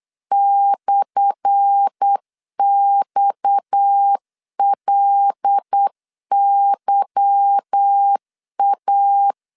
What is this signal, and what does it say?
Morse code, heard as a keyed tone, about 8 wpm, 800 Hz. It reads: /XLYA
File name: LEZIONI / 10 SIGNS / 10A SLASH 8wpm 3452